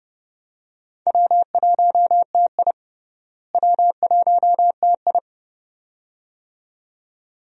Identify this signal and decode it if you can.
Morse code, heard as a keyed tone, about 31 wpm, 700 Hz. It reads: W1TI W1TI
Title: USA Call Signs - 4 Characters - Repeated 2x - 30wpm